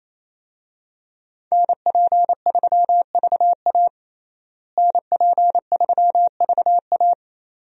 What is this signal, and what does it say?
Morse code, heard as a keyed tone, about 28 wpm, 700 Hz. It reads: NP3VA NP3VA
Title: USA Call Signs - 5 Characters - Repeated 2x - 28wpm